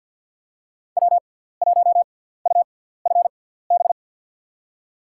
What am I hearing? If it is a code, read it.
Morse code, heard as a keyed tone, about 46 wpm, spaced as for 17 wpm, 700 Hz. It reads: W1UFB
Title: USA Call Signs - 5 Characters - 6x Character Spacing - 50wpm